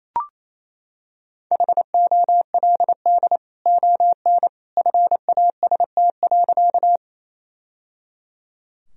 Morse code, heard as a keyed tone, about 28 wpm, 700 Hz. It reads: HOLD ON FAST.
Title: Sentences from Top 1000 Words - Review Part 2 - 28wpm